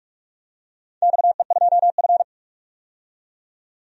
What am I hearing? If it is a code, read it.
Morse code, heard as a keyed tone, about 44 wpm, 690 Hz. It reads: XE2F
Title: DXCC-100 Call Signs - 4 Characters - 45wpm